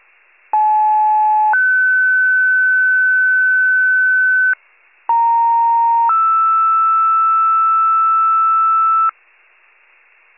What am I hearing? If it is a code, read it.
Two-tone page, heard as a one-second tone, then a three-second tone, about 860 then 1500 Hz, about 930 then 1320 Hz. It is heard from 0.5 to 9.1 s.